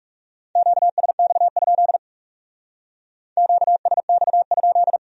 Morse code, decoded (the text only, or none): QSX? QSX?